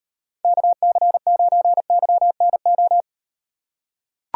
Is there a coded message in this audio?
Yes: Morse code from 0.4 to 3.0 s, heard as a keyed tone, about 38 wpm, 700 Hz. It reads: KC9YNO